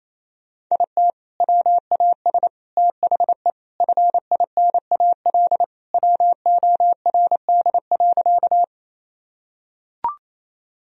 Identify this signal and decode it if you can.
Morse code, heard as a keyed tone, about 28 wpm, 700 Hz. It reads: IT WAS THE FINAL WORD.
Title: Sentences from Top 1000 Words - Review Part 2 - 28wpm